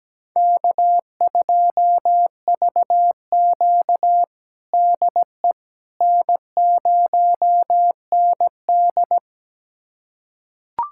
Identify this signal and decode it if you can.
Morse code, heard as a keyed tone, about 17 wpm, 700 Hz. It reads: K2VQ DE N0ND
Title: USA Callsign DE Callsign - 4 Characters - 17wpm